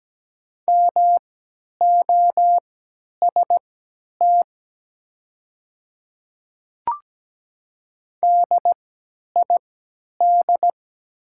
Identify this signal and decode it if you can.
Morse code, heard as a keyed tone, about 17 wpm, spaced as for 10 wpm, 700 Hz. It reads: MOST DID
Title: 3x Character Spacing - Top 200 Words - 17wpm